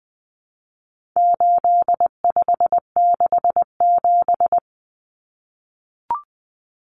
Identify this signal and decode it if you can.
Morse code, heard as a keyed tone, about 20 wpm, 700 Hz. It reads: 8567
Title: Numbers - 4-Digit 20wpm